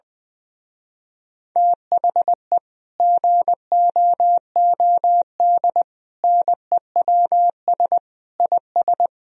Morse code, heard as a keyed tone, about 20 wpm, 700 Hz. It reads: THE GOOD NEWS IS
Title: Speed-Racing - Sets of 5 Words 20wpm